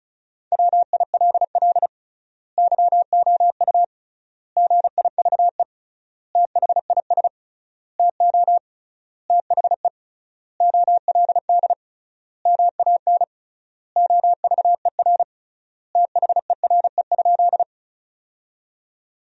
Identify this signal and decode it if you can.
Morse code, heard as a keyed tone, about 36 wpm, 700 Hz. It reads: WILL YOU GIVE THIS TO THE OLD MAN OVER THERE?
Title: 3x Word-Spacing - Sentences from Top 200 Words - 35wpm